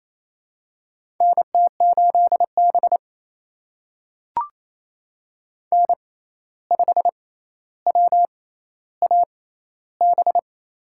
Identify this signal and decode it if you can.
Morse code, heard as a keyed tone, about 28 wpm, spaced as for 13 wpm, 700 Hz. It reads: NT8B N5WAB
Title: USA Call Signs - 6x Character Spacing - 28wpm